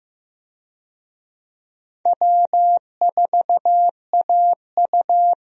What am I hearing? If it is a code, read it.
Morse code, heard as a keyed tone, about 15 wpm, 700 Hz. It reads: W4AU
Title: USA Call Signs - 4 Characters - Repeated 2x - 15wpm